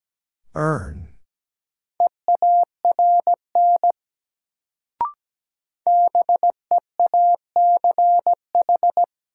Morse code, heard as a keyed tone, about 17 wpm, 700 Hz. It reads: EARN BEACH
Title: Sentences from Top 1500 Words - New Word Drill Alt - 17wpm